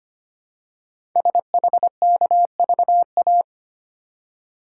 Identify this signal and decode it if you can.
Morse code, heard as a keyed tone, about 25 wpm, 700 Hz. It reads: SHKVA
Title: Code Groups - 5 Letters - 25wpm